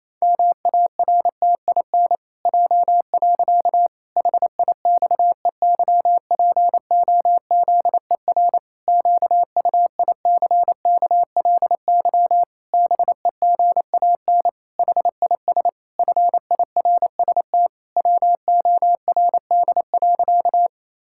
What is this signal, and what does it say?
Morse code, heard as a keyed tone, about 28 wpm, 700 Hz. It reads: MARTIN J. HIXEYPOZER QUICKLY BEGAN HIS FIRST WORD.
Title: Pangram Challenge v2 - 28wpm